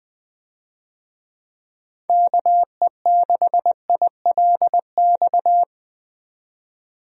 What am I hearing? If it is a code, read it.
Morse code, heard as a keyed tone, about 20 wpm, 700 Hz. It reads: KE6ILX
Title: USA Call Signs - 6 Characters - 20wpm